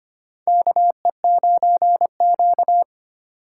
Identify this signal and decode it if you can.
Morse code, heard as a keyed tone, about 25 wpm, 700 Hz. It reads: KE9Q
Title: USA Call Signs - 4 Characters - 3x Character Spacing - 25wpm